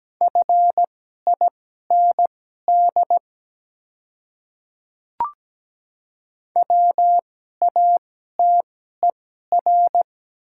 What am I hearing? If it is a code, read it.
Morse code, heard as a keyed tone, about 17 wpm, spaced as for 12 wpm, 700 Hz. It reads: FIND WATER
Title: 2x Character Spacing - Top 100 Words - 17wpm